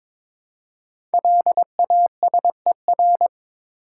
Morse code, heard as a keyed tone, about 22 wpm, 700 Hz. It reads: LASER
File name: 5 Letter Words - First 3 Letters Given - 22wpm